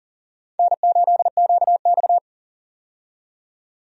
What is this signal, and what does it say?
Morse code, heard as a keyed tone, about 39 wpm, 700 Hz. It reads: N8QX
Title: USA Call Signs - 4 Characters - 40wpm